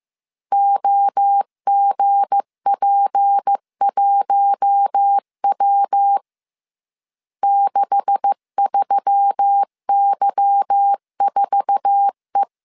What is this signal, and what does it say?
Morse code, heard as a keyed tone, about 15 wpm, 800 Hz. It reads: OGP1W 63Y4E
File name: CW / DET / 1 9 1500C S A 14wpm 3222